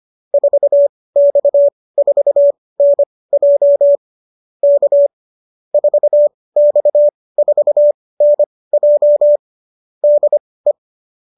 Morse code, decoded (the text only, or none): 4X4NJ K 4X4NJ DE